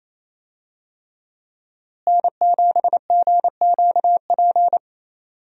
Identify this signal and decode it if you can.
Morse code, heard as a keyed tone, about 28 wpm, 700 Hz. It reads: N7GQP
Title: USA Call Signs - 5 Characters - Repeated 3x - 28wpm